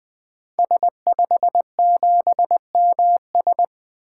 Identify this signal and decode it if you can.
Morse code, heard as a keyed tone, about 20 wpm, 700 Hz. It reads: S57MS
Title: 2023 SOTA Activator Call Signs - 20wpm